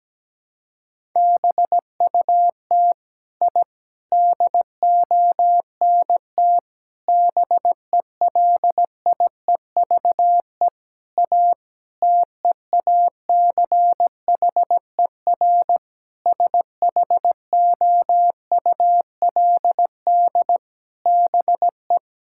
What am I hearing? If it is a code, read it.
Morse code, heard as a keyed tone, about 17 wpm, 700 Hz. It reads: BUT I DONT BELIEVE A TEACHER SHOULD BE
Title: Sentences from Top 1000 Words - Review Part 6 - 17wpm